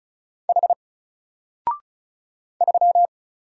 Morse code, heard as a keyed tone, about 34 wpm, 700 Hz. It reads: H 3
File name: Single-Letter-Number-35wpm